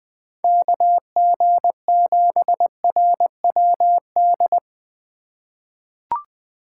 Morse code, heard as a keyed tone, about 20 wpm, 700 Hz. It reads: KG7RWD
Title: USA Call Signs - 2x Character Spacing - 20wpm